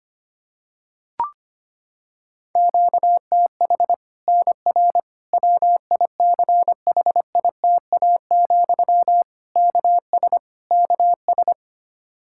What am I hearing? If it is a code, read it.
Morse code, heard as a keyed tone, about 25 wpm, 700 Hz. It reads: QTH NR WICHITA, KS KS
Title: Practice QSO - Exchange 1 - Beginning - 25wpm